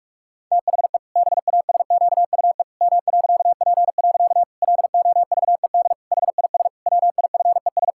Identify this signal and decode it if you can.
Morse code, heard as a keyed tone, about 46 wpm, 700 Hz. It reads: THE BASQUE M.P. LOVED HIS WIFES
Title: Pangram Challenge v2 - 45wpm